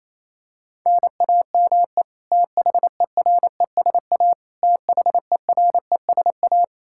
Morse code, heard as a keyed tone, about 28 wpm, 700 Hz. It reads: NAME THERESA THERESA
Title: Practice QSO - Exchange 1 - Beginning - 28wpm